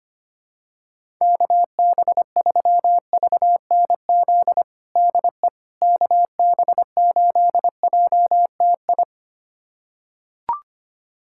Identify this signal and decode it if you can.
Morse code, heard as a keyed tone, about 25 wpm, 700 Hz. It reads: KB3VNZ DE KB8JTI
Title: USA Callsign DE Callsign 25wpm